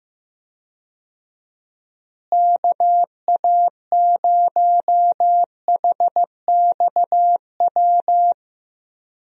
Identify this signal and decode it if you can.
Morse code, heard as a keyed tone, about 15 wpm, 700 Hz. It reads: KA0HXW
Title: USA Call Signs - 6 Characters - Repeated 2x - 15wpm